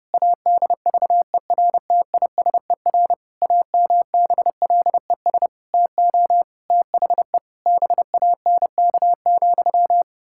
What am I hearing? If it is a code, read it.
Morse code, heard as a keyed tone, about 30 wpm, 700 Hz. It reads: ADVERTISER AMBLES TO THE BANK,
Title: Pangram Challenge v2 - 30wpm